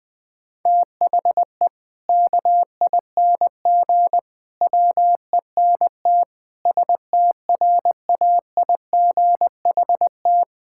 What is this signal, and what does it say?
Morse code, heard as a keyed tone, about 20 wpm, 700 Hz. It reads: THE KING WENT STRAIGHT
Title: Sentences from Top 1000 Words - Review Part 6 - 20wpm